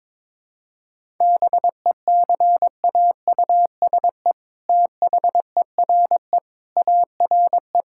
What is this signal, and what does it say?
Morse code, heard as a keyed tone, about 21 wpm, 700 Hz. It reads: BECAUSE THERE ARE
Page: morse-code-ninja-practice-22wpm